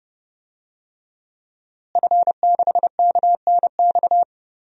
Morse code, heard as a keyed tone, about 31 wpm, 700 Hz. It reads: F6KNX
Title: DXCC-100 Call Signs - 5 Characters - Repeated 3x - 30wpm